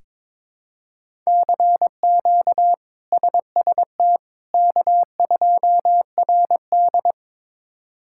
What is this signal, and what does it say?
Morse code, heard as a keyed tone, about 22 wpm, 700 Hz. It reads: CQ SST K2RD
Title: K1USN SST QSO Practice - 22wpm